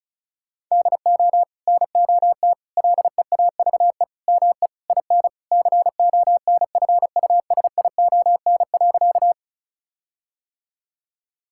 Morse code, heard as a keyed tone, about 36 wpm, 700 Hz. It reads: DO NOT LEAVE ME IN CONFUSION.
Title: Sentences from Top 1600 Words 35wpm